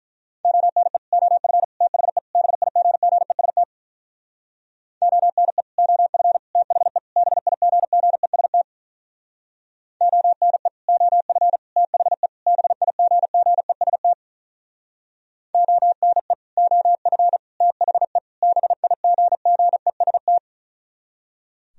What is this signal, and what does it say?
Morse code, heard as a keyed tone, about 47 wpm, 700 Hz. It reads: ONE OF THE BIGGEST ONE OF THE BIGGEST ONE OF THE BIGGEST ONE OF THE BIGGEST